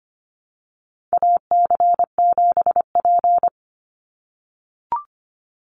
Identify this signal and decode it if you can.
Morse code, heard as a keyed tone, about 25 wpm, 700 Hz. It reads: AC7P
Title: NA SOTA Chaser Callsigns 25wpm